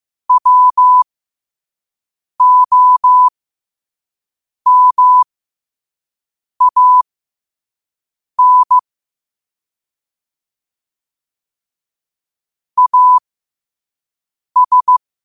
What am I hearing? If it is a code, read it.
Morse code, heard as a keyed tone, about 15 wpm, spaced as for 5 wpm, 1000 Hz. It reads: WOMAN AS
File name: qotdfiles / 5wpm